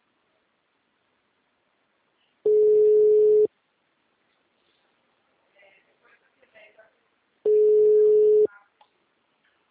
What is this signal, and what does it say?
Ringback tone (Europe, 425 Hz), with speech.